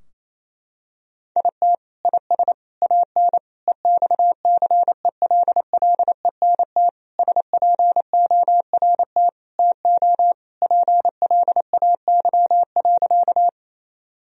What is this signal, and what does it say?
Morse code, heard as a keyed tone, about 27 wpm, 700 Hz. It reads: IT IS AN EXCELLENT SPORT TO PLAY.
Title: Sentences from Top 1000 Words - Review Part 4 - 28wpm